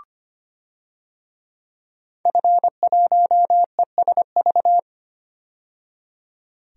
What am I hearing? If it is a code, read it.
Morse code, heard as a keyed tone, about 25 wpm, 700 Hz. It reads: F1ESV
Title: DXCC-100 Call Signs - 5 Characters - 25wpm